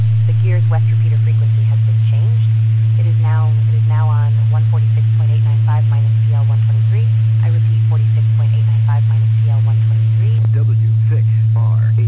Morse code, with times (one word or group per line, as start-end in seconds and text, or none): none